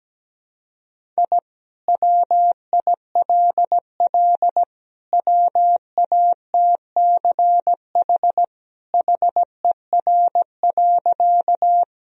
Morse code, heard as a keyed tone, about 17 wpm, 700 Hz. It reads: I WILL WATCH HER.
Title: Sentences from Top 1000 Words - Review Part 1 - 17wpm